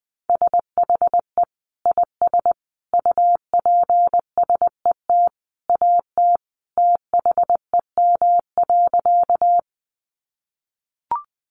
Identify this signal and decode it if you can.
Morse code, heard as a keyed tone, about 20 wpm, 700 Hz. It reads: SHE IS UPSET AT THEM.
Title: Sentences from Top 800 Words 20wpm